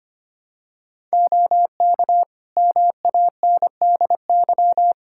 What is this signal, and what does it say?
Morse code, heard as a keyed tone, about 25 wpm, 700 Hz. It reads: OK MANDY
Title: Practice QSO - Exchange 3 - More Info - 25wpm